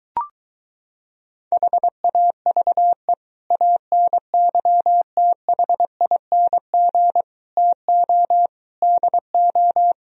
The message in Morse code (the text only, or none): HAVE ANYTHING TO DO